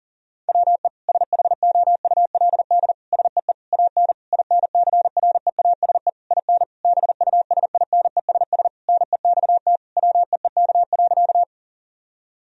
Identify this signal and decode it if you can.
Morse code, heard as a keyed tone, about 40 wpm, 700 Hz. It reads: WE SHOULD SEE AN INCREASE IN BUSINESS NEXT WEEK.
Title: Sentences from Top 1000 Words - Review Part 4 - 40wpm